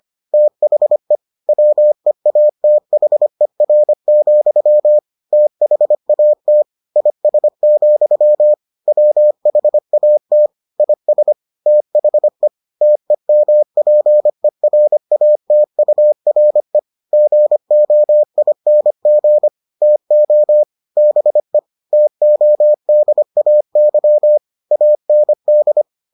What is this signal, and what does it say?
Morse code, heard as a keyed tone, about 25 wpm, 600 Hz. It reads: THE WEATHER, THAT IS, WHAT IS THE TEMPERATURE GOING TO BE TODAY AND